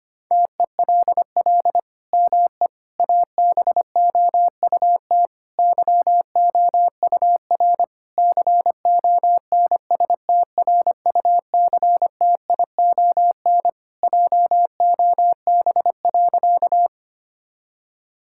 Morse code, heard as a keyed tone, about 25 wpm, 700 Hz. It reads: TELL ME ABOUT YOUR CONSTRUCTION JOB.
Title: Sentences from Top 1600 Words 25wpm